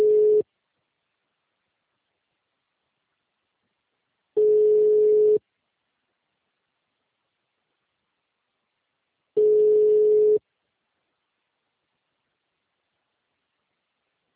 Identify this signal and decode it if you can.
Ringback tone (Europe, 425 Hz).
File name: out- -2016- - - .1280